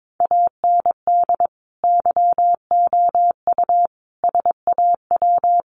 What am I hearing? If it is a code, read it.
Morse code, heard as a keyed tone, about 22 wpm, 700 Hz. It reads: AND YOU SAW
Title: Sentences from Top 1300 Words 22wpm